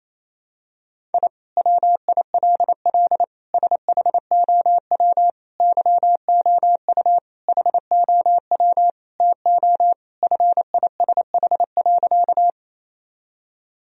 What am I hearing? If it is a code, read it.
Morse code, heard as a keyed tone, about 28 wpm, 700 Hz. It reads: I WILL SHOW YOU HOW TO FISH.
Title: Sentences from Top 1000 Words - Review Part 1 - 28wpm